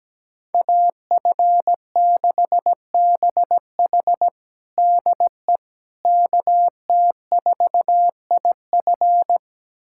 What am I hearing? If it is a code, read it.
Morse code, heard as a keyed tone, about 17 wpm, 700 Hz. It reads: AF6BH DE KT4IF
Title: USA Callsign DE Callsign - 5 Characters - 17wpm